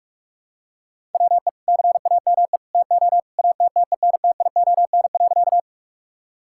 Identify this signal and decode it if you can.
Morse code, heard as a keyed tone, about 41 wpm, 700 Hz. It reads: WE CAME TO ATTENTION.
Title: Sentences from Top 1000 Words - Review Part 6 - 45wpm